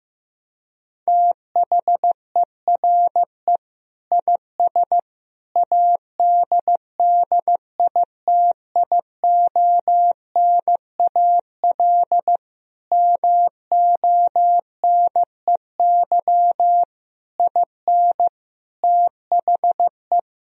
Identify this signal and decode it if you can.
Morse code, heard as a keyed tone, about 15 wpm, 700 Hz. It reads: THERE IS ADDITIONAL MONEY IN THE
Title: Sentences from Top 1000 Words - Review Part 4 - 15wpm